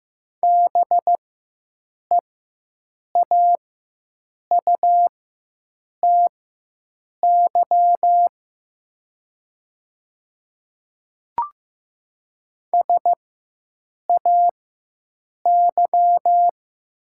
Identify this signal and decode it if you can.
Morse code, heard as a keyed tone, about 16 wpm, spaced as for 7 wpm, 700 Hz. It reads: BEAUTY SAY
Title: 4x Character Spacing - Top 500 Words - 15wpm